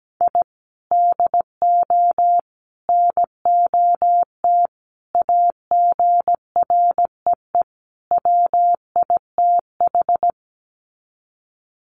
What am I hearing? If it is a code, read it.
Morse code, heard as a keyed tone, about 17 wpm, 700 Hz. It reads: I DO NOT AGREE WITH
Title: Sets of 5 Words Encore 17wpm